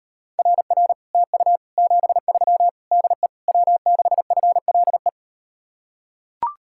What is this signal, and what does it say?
Morse code, heard as a keyed tone, about 38 wpm, 700 Hz. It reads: RR TU 73 DE W6FLE